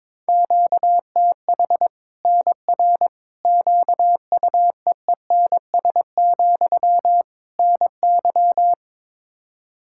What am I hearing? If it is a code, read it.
Morse code, heard as a keyed tone, about 22 wpm, 700 Hz. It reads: QTH NR QUEENS, NY
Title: Practice QSO - Exchange 1 - Beginning - 22wpm